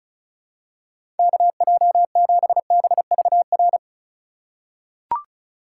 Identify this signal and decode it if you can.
Morse code, heard as a keyed tone, about 36 wpm, 700 Hz. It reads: KJ7BVR
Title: USA Call Signs - 6 Characters - 4x Character Spacing - 35wpm